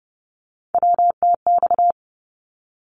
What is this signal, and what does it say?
Morse code, heard as a keyed tone, about 28 wpm, 700 Hz. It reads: WTX
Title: ARRL Sections 30wpm